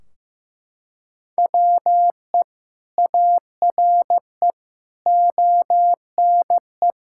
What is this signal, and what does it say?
Morse code, heard as a keyed tone, about 15 wpm, 700 Hz. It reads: WE ARE ONE
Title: Sentences from Top 1600 Words 15wpm